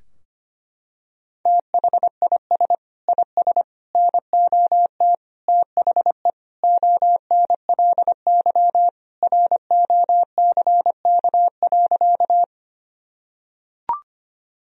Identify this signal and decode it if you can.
Morse code, heard as a keyed tone, about 25 wpm, 700 Hz. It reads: THIS IS NOT THE ONLY ROCK.
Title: Sentences from Top 1000 Words - Review Part 2 - 25wpm